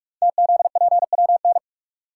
Morse code, heard as a keyed tone, about 44 wpm, 700 Hz. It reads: TZPWN
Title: Code Groups - 5 Letter:Numbers - 45wpm